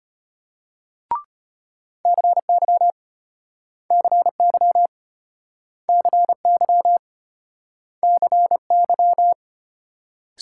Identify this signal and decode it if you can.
Morse code, heard as a keyed tone, about 35 wpm, 700 Hz. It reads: CY CY CY CY